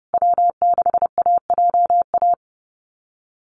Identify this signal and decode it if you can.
Morse code, heard as a keyed tone, about 30 wpm, 700 Hz. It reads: W6AJA
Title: USA Call Signs Repeated 2x 30wpm